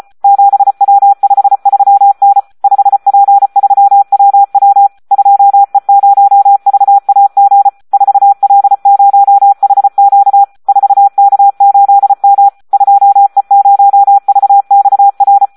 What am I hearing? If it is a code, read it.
Morse code, heard as a keyed tone, about 34 wpm, 800 Hz. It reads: 7W53N 5P3WW 2E0VAG 4L0HQ 4K8M 2E0VXR
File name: Audio practica CW / SD 1 34